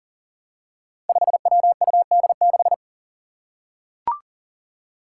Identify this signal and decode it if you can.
Morse code, heard as a keyed tone, about 40 wpm, 700 Hz. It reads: 5WUD6
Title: Five Challenging Letter-Number Combinations 40wpm